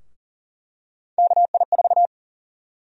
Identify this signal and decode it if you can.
Morse code, heard as a keyed tone, about 40 wpm, 700 Hz. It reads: KI4
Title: USA Call Sign Prefixes - 40wpm